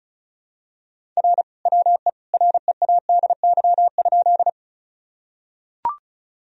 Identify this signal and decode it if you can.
Morse code, heard as a keyed tone, about 36 wpm, 700 Hz. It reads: R WE READY?
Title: Sentences from Top 400 Words and Abbreviations 35wpm